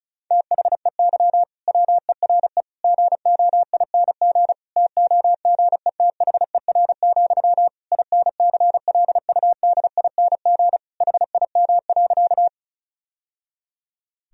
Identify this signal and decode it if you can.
Morse code, heard as a keyed tone, about 36 wpm, 700 Hz. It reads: THEY WERE GOING TOGETHER, INCLUDING HIM.
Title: Sentences from Top 1000 Words - Review Part 4 - 35wpm